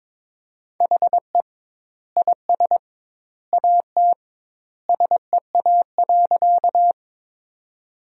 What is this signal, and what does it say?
Morse code, heard as a keyed tone, about 22 wpm, 700 Hz. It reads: HE IS AT SEA.
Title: 2x Word-Spacing - Sentences from Top 300 Words - 22wpm